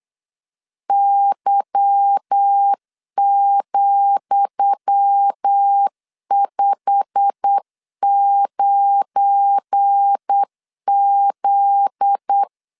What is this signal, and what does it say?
Morse code, heard as a keyed tone, about 8 wpm, 800 Hz. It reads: Y,59Z
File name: LEZIONI / 10 SIGNS / 10D COMA 8wpm 3749